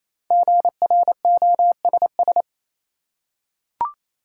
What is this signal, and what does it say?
Morse code, heard as a keyed tone, about 28 wpm, 700 Hz. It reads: GROSS
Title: Five Letter Words - 28wpm